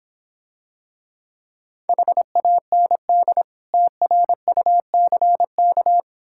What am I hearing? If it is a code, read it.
Morse code, heard as a keyed tone, about 26 wpm, 700 Hz. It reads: HAND TRUCK